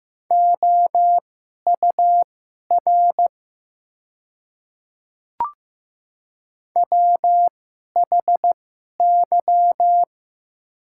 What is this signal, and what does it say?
Morse code, heard as a keyed tone, about 15 wpm, spaced as for 11 wpm, 700 Hz. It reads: OUR WHY
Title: 2x Character Spacing - Top 200 Words - 15wpm